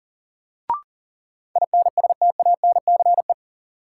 Morse code, heard as a keyed tone, about 40 wpm, 700 Hz. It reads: INSTANCE
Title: Sentences from Top 800 Words - New Word Drill - 40wpm